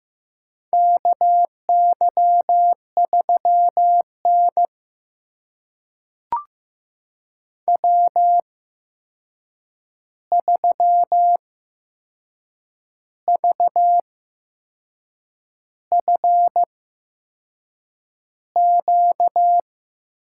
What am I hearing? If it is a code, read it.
Morse code, heard as a keyed tone, about 15 wpm, spaced as for 6 wpm, 700 Hz. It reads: KY3N W3VFQ
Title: USA Call Signs - 8x Character Spacing - 15wpm